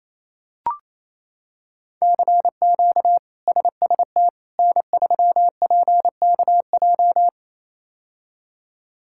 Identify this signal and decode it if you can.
Morse code, heard as a keyed tone, about 27 wpm, 700 Hz. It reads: CQ SST N3PKJ
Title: K1USN SST QSO Practice - 28wpm